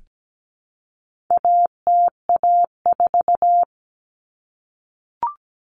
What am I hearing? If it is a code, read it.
Morse code, heard as a keyed tone, about 17 wpm, 700 Hz. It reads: ATA4
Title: Common Cut Numbers - 4-Digit 17wpm